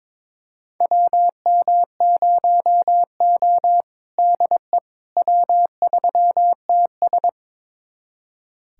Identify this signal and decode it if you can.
Morse code, heard as a keyed tone, about 22 wpm, 700 Hz. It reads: WM0O DE W3TS
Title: USA Callsign DE Callsign - 4 Characters - 22wpm